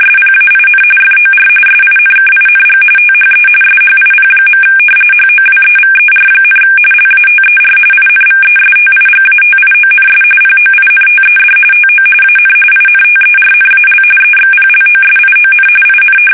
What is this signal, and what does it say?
RTTY, about 45.45 baud. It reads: SPAIN FOR SINKING THE SHIP. HOWEVER, LATER INVESTIGATIONS POINTED TO AN INTERNAL EXPLOSION IN
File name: fsk850 / usnavy3